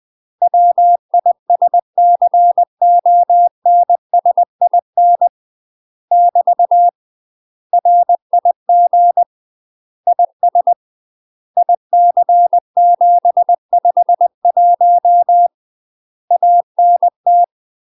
Morse code, heard as a keyed tone, about 20 wpm, 700 Hz. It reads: WISCONSIN <BT> RIG IS IC751 ANT
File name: UK MASTER QSOs TRAINING / 20WPM / UK MASTER QSOs 0043